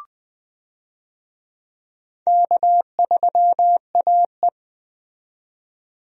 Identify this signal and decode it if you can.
Morse code, heard as a keyed tone, about 20 wpm, 700 Hz. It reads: K3AE
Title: USA Call Signs - 4 Characters - 20wpm